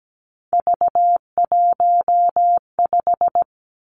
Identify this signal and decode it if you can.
Morse code, heard as a keyed tone, about 17 wpm, 700 Hz. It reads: V15
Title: Three Challenging Letter-Number Combinations 17wpm